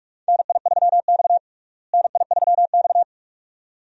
Morse code, heard as a keyed tone, about 46 wpm, 700 Hz. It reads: NI3X NI3X
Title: USA Call Signs - 4 Characters - Repeated 2x - 45wpm